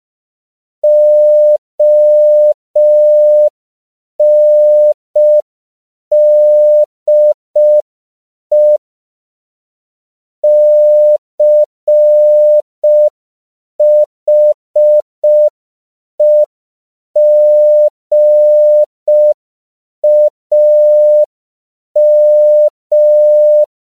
Morse code, heard as a keyed tone, about 5 wpm, 600 Hz. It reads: ONDE CHEGAM
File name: CW / 5PPM